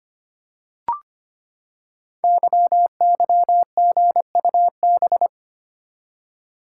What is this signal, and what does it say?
Morse code, heard as a keyed tone, about 25 wpm, 700 Hz. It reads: YYGUB
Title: Code Groups - 5 Letters - 25wpm